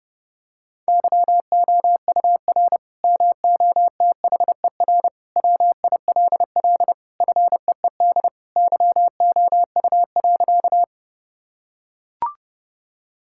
Morse code, heard as a keyed tone, about 31 wpm, 700 Hz. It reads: YOUR MOTHER WILL FEED YOU.
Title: Sentences from Top 1600 Words 30wpm